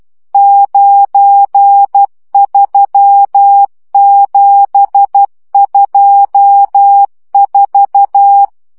Morse code, heard as a keyed tone, about 12 wpm, 800 Hz. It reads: 93724
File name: morze vizsga 2011-2 10